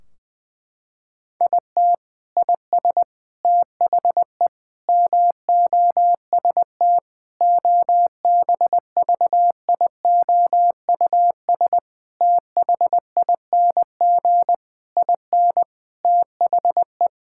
Morse code, heard as a keyed tone, about 21 wpm, 700 Hz. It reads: IT IS THE MOST OBVIOUS THING IN THE
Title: Sentences from Top 1600 Words 20wpm